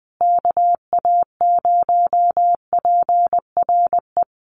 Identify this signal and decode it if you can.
Morse code, heard as a keyed tone, about 20 wpm, 700 Hz. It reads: KA0PRE
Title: USA Call Signs Repeated 3x 20wpm